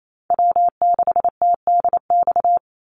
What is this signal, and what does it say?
Morse code, heard as a keyed tone, about 28 wpm, 700 Hz. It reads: W6TDX
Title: NA SOTA Chaser Callsigns 28wpm